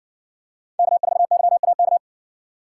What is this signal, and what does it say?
Morse code, heard as a keyed tone, about 60 wpm, 700 Hz. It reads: K4YAK